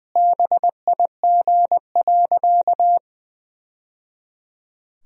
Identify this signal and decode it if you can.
Morse code, heard as a keyed tone, about 21 wpm, 700 Hz. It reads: BIG.
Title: Sentences from Top 1000 Words - Review Part 2 - 20wpm